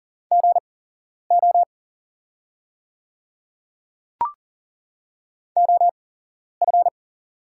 Morse code, heard as a keyed tone, about 40 wpm, spaced as for 11 wpm, 700 Hz. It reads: GO OF
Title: 8x Character Spacing - Top 100 Words - 40wpm